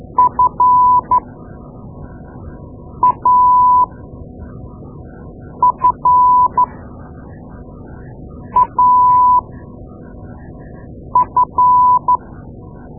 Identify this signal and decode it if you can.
Morse code, heard as a keyed tone, about 12 wpm, spaced as for 4 wpm, 1000 Hz. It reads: FAFAF
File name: Morse Lessons / Morse Lessons / Tape-2A